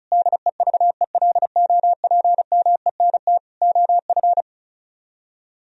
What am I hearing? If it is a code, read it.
Morse code, heard as a keyed tone, about 45 wpm, 700 Hz. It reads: DEVELOPMENT OF